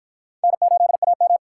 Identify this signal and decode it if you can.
Morse code, heard as a keyed tone, about 53 wpm, 700 Hz. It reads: N8AM